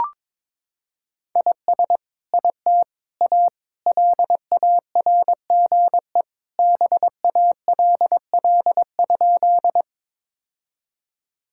Morse code, heard as a keyed tone, about 22 wpm, 700 Hz. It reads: IS IT A LARGE BALL?
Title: Sentences from Top 1000 Words - Review Part 6 - 22wpm